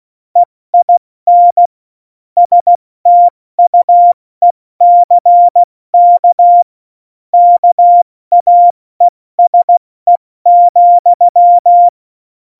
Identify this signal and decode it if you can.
Morse code, heard as a keyed tone, about 16 wpm, 700 Hz. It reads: EIN STUECK KAESE,
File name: Morsekurs / Witze / Witz 16. Januar 2018, 70 F80